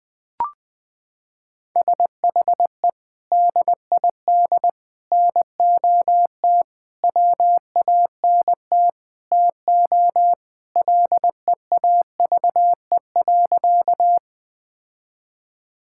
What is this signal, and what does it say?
Morse code, heard as a keyed tone, about 20 wpm, 700 Hz. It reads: SHE DID NOT WANT TO LEAVE.
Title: Sentences with 4-Word Phrases - 20wpm